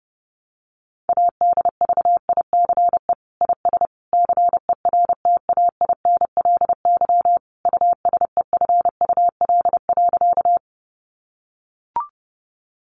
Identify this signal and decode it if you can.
Morse code, heard as a keyed tone, about 30 wpm, 700 Hz. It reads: ADVICE IS CERTAINLY USEFUL.
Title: Sentences from Top 1100 Words 30wpm